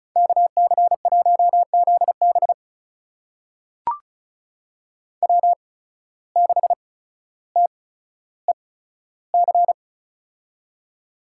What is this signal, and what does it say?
Morse code, heard as a keyed tone, about 36 wpm, spaced as for 15 wpm, 700 Hz. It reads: KC1ZB W6TEC
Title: USA Call Signs - 5 Characters - 8x Character Spacing - 35wpm